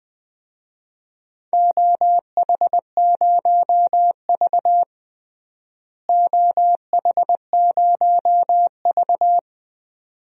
Morse code, heard as a keyed tone, about 21 wpm, 700 Hz. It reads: OH0V OH0V
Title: DXCC-100 Call Signs - 4 Characters - Repeated 2x - 20wpm